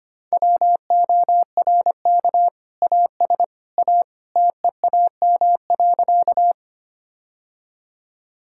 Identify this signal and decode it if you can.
Morse code, heard as a keyed tone, about 25 wpm, 700 Hz. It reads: WORK AS A TEAM.
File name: practice / 25wpm / Sentences with 2-Word Phrases - 25wpm